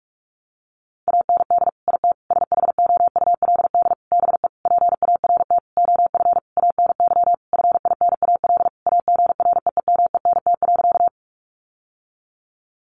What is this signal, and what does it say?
Morse code, heard as a keyed tone, about 44 wpm, 700 Hz. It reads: AND IT SHOULD BE PART OF ANY FINAL AGREEMENT.
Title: Sentences from Top 1500 Words 45wpm